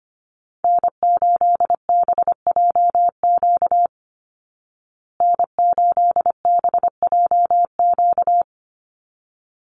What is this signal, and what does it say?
Morse code, heard as a keyed tone, about 25 wpm, 700 Hz. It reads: N8BJQ N8BJQ
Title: Call Signs Repeated 3x - 25wpm